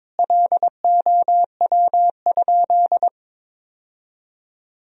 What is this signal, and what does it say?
Morse code, heard as a keyed tone, about 22 wpm, 700 Hz. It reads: LOW?
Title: Sentences from Top 1000 Words - Review Part 1 - 22wpm